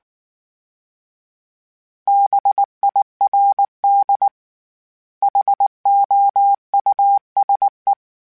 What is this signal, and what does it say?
Morse code, heard as a keyed tone, about 19 wpm, 800 Hz. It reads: BIRD HOUSE